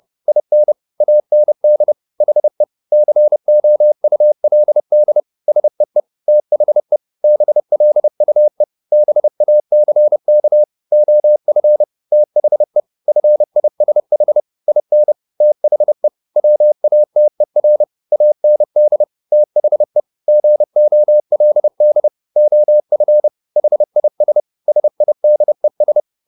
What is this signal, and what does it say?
Morse code, heard as a keyed tone, about 30 wpm, 600 Hz. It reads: IN AND HE COULD SEE THE BLUE BACK OF THE FISH IN THE WATER AND THE GOLD OF HIS SIDES
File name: the old man and the sea / the old man and the sea 1 30wpm